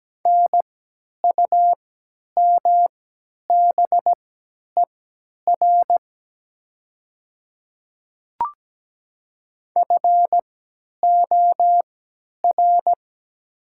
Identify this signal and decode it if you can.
Morse code, heard as a keyed tone, about 17 wpm, spaced as for 10 wpm, 700 Hz. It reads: NUMBER FOR
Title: 3x Character Spacing - Top 100 Words - 17wpm